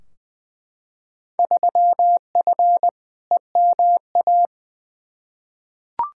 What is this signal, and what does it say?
Morse code, heard as a keyed tone, about 20 wpm, 700 Hz. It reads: 3F EMA
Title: Field Day Exchanges Encore - 20wpm